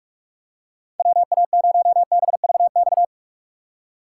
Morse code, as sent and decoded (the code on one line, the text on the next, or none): .-- .- ----- -... ...- -..-
WA0BVX